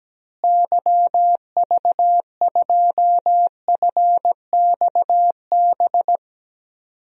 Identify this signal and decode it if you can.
Morse code, heard as a keyed tone, about 17 wpm, 700 Hz. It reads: YV2FXB
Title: DXCC-100 Call Signs - 6 Characters - Repeated 3x - 17wpm